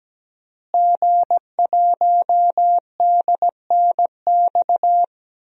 Morse code, heard as a keyed tone, about 17 wpm, 700 Hz. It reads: G1DNX
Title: DXCC-100 Call Signs - 5 Characters - Repeated 3x - 17wpm